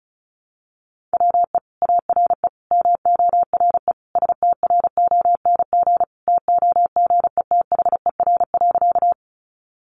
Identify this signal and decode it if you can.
Morse code, heard as a keyed tone, about 34 wpm, 700 Hz. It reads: WE ARE MORE STRONG TOGETHER.
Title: Sentences from Top 500 Words 35wpm